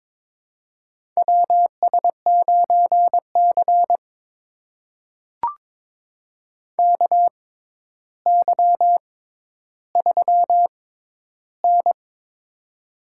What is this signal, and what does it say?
Morse code, heard as a keyed tone, about 22 wpm, spaced as for 11 wpm, 700 Hz. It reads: WS9C KY3N
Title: USA Call Signs - 4 Characters - 6x Character Spacing - 22wpm